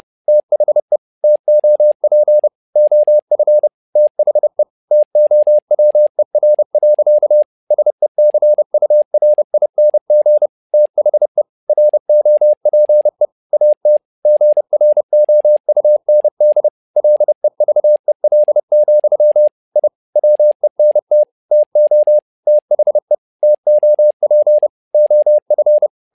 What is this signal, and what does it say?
Morse code, heard as a keyed tone, about 30 wpm, 600 Hz. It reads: THE TOP OF THE TOWER. SECURING THE ROPE AT GROUND LEVEL, I WENT TO THE TOP OF
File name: learning center / tower trouble / tower trouble 30wpm